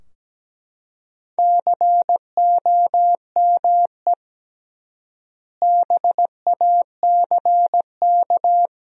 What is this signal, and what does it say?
Morse code, heard as a keyed tone, about 17 wpm, 700 Hz. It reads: COME BACK
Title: 3x Word-Spacing - Sentences from Top 500 Words - 17wpm